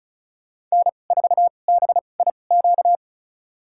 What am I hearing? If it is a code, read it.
Morse code, heard as a keyed tone, about 36 wpm, spaced as for 25 wpm, 700 Hz. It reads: N4BIQ
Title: USA Call Signs - 5 Characters - 2x Character Spacing - 35wpm